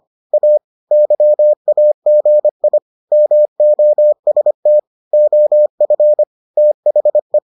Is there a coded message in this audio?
Yes: Morse code from 0.3 to 7.4 s, heard as a keyed tone, about 25 wpm, 600 Hz. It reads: A YAGI MOST OF THE